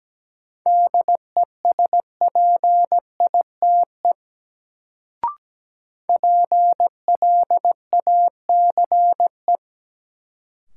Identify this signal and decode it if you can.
Morse code, heard as a keyed tone, about 17 wpm, 700 Hz. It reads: DESPITE PLACE
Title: Sets of 4 Words - New Word Drill - 17wpm